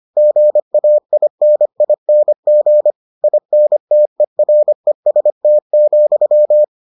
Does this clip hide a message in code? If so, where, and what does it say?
Yes: Morse code from 0.2 to 6.6 s, heard as a keyed tone, about 25 wpm, 600 Hz. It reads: GAINING INTEREST,